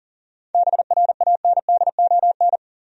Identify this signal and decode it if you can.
Morse code, heard as a keyed tone, about 40 wpm, 700 Hz. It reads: BRANDON
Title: USA Common Names - 40wpm